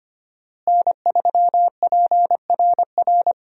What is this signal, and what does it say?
Morse code, heard as a keyed tone, about 25 wpm, 700 Hz. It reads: N3PRR
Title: USA Call Signs - 5 Characters - Repeated 3x - 25wpm